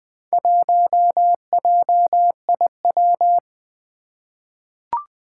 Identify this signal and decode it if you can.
Morse code, heard as a keyed tone, about 21 wpm, 700 Hz. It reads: 1JIW
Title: Four Challenging Letter-Number Combinations 20wpm